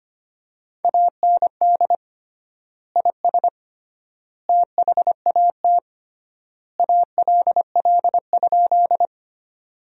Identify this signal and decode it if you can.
Morse code, heard as a keyed tone, about 25 wpm, 700 Hz. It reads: AND IS THAT ALL?
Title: 3x Word-Spacing - Sentences from Top 100 Words - 25wpm